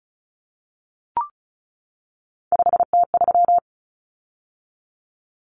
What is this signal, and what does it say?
Morse code, heard as a keyed tone, about 34 wpm, 700 Hz. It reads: 5T3
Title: Common Cut Numbers - 3-Digit 35wpm